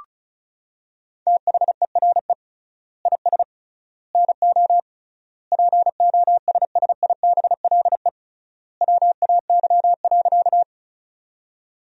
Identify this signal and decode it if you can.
Morse code, heard as a keyed tone, about 34 wpm, 700 Hz. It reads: THERE IS NO POSSIBLE WAY.
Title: 3x Word-Spacing - Sentences from Top 500 Words - 35wpm